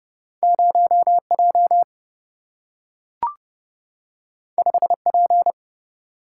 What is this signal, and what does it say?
Morse code, heard as a keyed tone, about 30 wpm, 700 Hz. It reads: 0J 5P
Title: Single-Letter Single-Number 30wpm